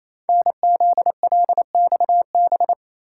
Morse code, heard as a keyed tone, about 27 wpm, 700 Hz. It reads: NZLXB
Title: Code Groups - 5 Letters - 28wpm